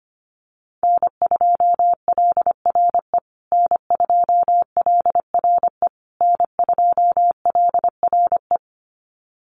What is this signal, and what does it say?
Morse code, heard as a keyed tone, about 25 wpm, 700 Hz. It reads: N2LRE N2LRE N2LRE
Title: USA Call Signs Repeated 3x 25wpm